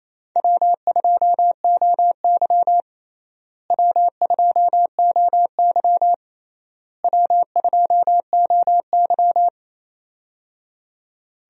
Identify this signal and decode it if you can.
Morse code, heard as a keyed tone, about 28 wpm, 700 Hz. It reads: W2OY W2OY W2OY
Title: USA Call Signs - 4 Characters - Repeated 3x - 28wpm